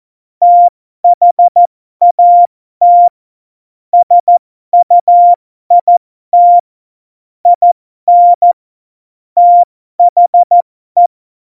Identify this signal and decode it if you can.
Morse code, heard as a keyed tone, about 14 wpm, spaced as for 12 wpm, 700 Hz. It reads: THAT SUIT IN THE
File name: Morsekurs / Witze